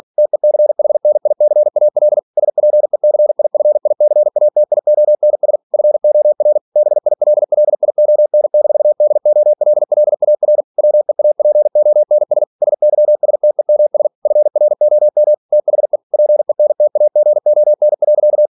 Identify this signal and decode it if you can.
Morse code, heard as a keyed tone, about 46 wpm, 600 Hz. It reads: TECHNICAL SPECIFICATIONS FOR BILLION-DOLLAR WEAPONS SYSTEMS FROM THE PENTAGON.
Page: Never 47wpm